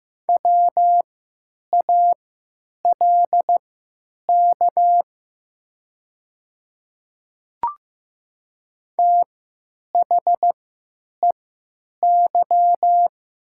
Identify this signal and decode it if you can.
Morse code, heard as a keyed tone, about 15 wpm, spaced as for 9 wpm, 700 Hz. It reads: WALK THEY
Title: 3x Character Spacing - Top 300 Words - 15wpm